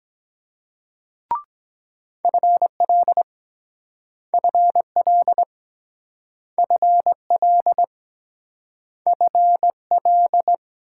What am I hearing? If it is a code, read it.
Morse code, heard as a keyed tone, about 23 wpm, 700 Hz. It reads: FL FL FL FL